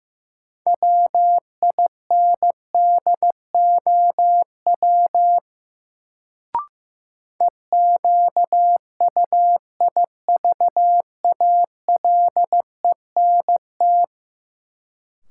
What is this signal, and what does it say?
Morse code, heard as a keyed tone, about 16 wpm, 700 Hz. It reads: WINDOW EQUIVALENT
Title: Sentences from Top 1600 Words - New Word Drill Alt - 15wpm